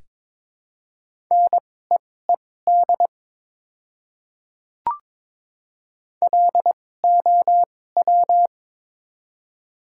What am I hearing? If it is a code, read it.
Morse code, heard as a keyed tone, about 22 wpm, spaced as for 16 wpm, 700 Hz. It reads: NEED LOW
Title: 2x Character Spacing - Top 200 Words - 22wpm